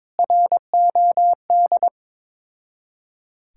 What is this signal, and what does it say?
Morse code, heard as a keyed tone, about 23 wpm, 700 Hz. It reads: ROD
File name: Difficult Words with Hints - 22wpm